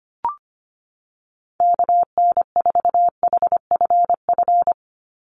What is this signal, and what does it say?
Morse code, heard as a keyed tone, about 25 wpm, 700 Hz. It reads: KN4HFF
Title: USA Call Signs 25wpm